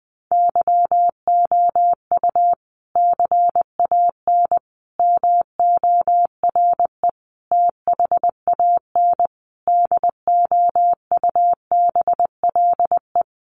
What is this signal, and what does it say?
Morse code, heard as a keyed tone, about 20 wpm, 700 Hz. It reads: YOU CAN MORE THAN DOUBLE
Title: Sentences from Top 1400 Words 20wpm